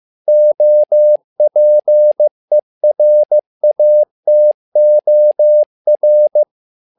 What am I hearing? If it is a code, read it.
Morse code, heard as a keyed tone, about 15 wpm, 600 Hz. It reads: OPERATOR
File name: learning center / tower trouble / tower trouble 15wpm